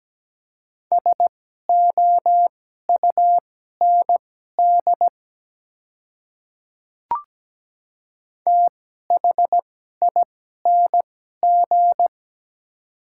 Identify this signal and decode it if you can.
Morse code, heard as a keyed tone, about 17 wpm, spaced as for 12 wpm, 700 Hz. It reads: SOUND THING
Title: 2x Character Spacing - Top 100 Words - 17wpm